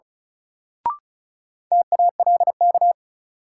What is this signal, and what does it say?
Morse code, heard as a keyed tone, about 36 wpm, 700 Hz. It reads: TALK
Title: Sets of 2 Words - New Word Drill - 35wpm